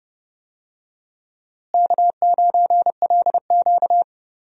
Morse code, heard as a keyed tone, about 32 wpm, 700 Hz. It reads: K9LQ
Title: USA Call Signs - 4 Characters - Repeated 3x - 30wpm